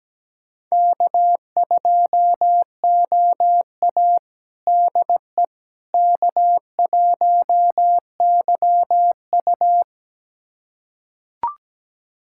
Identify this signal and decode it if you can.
Morse code, heard as a keyed tone, about 17 wpm, 700 Hz. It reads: K2OA DE K1YU
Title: USA Callsign DE Callsign - 4 Characters - 17wpm